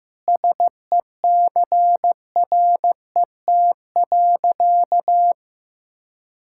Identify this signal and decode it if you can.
Morse code, heard as a keyed tone, about 15 wpm, 700 Hz. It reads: SECRET.
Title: Sentences from Top 1600 Words 15wpm